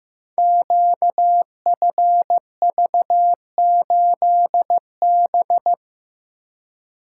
Morse code, heard as a keyed tone, about 15 wpm, 700 Hz. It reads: QFV8B
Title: Code Groups - 5 Letter:Numbers - 15wpm